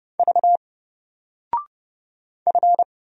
Morse code, heard as a keyed tone, about 29 wpm, 700 Hz. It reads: V F
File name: Single-Letter-Number-30wpm